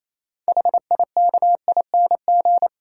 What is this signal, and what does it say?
Morse code, heard as a keyed tone, about 28 wpm, 690 Hz. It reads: HIKING